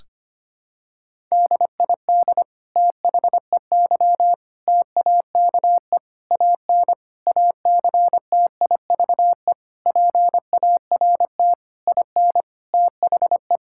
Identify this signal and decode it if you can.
Morse code, heard as a keyed tone, about 25 wpm, 700 Hz. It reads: DID THEY TAKE AN ACTIVE PART IN THE
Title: Sentences from Top 1000 Words - Review Part 6 - 25wpm